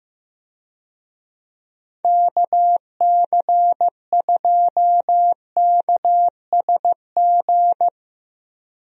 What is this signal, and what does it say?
Morse code, heard as a keyed tone, about 16 wpm, 700 Hz. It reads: KC2KSG
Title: USA Call Signs - 6 Characters - 15wpm